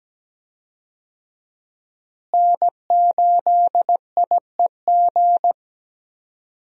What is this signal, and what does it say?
Morse code, heard as a keyed tone, about 17 wpm, 700 Hz. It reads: N8IEG
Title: USA Call Signs - 5 Characters - Repeated 2x - 17wpm